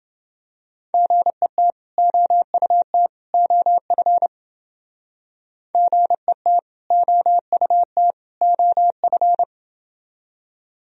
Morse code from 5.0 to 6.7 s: GET

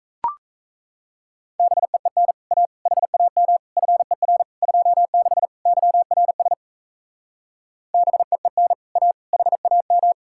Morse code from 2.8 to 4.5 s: HAM FER

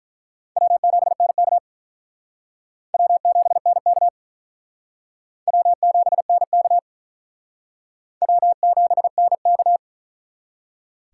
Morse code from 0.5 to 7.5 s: W7NK W7NK W7NK

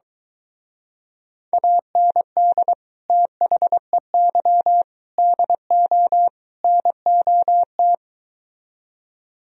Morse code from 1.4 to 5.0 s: AND THEY

